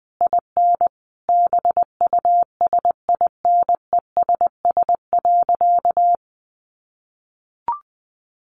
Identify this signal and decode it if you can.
Morse code, heard as a keyed tone, about 20 wpm, 700 Hz. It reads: IN BUSINESS.
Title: Sentences from Top 1300 Words 20wpm